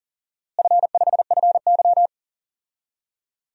Morse code, heard as a keyed tone, about 40 wpm, 700 Hz. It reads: F5FY
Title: DXCC-100 Call Signs - 4 Characters - Repeated 2x - 40wpm